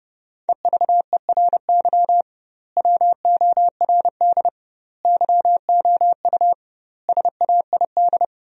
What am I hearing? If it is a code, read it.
Morse code, heard as a keyed tone, about 30 wpm, 700 Hz. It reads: EVERY WORD YOU SAID
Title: 2x Word-Spacing - Sentences from Top 400 Words - 30wpm